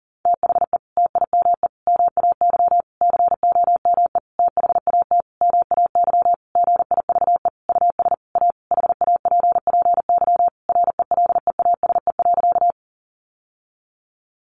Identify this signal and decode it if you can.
Morse code, heard as a keyed tone, about 40 wpm, 700 Hz. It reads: THE TIME MAY COME THAT MAY GIVE US A HAPPY RELEASE.
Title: Sentences from Top 1100 Words 40wpm